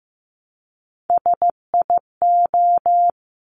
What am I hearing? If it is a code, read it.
Morse code, heard as a keyed tone, about 15 wpm, 700 Hz. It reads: SIO
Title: Top Three Letter Combinations in Words 15wpm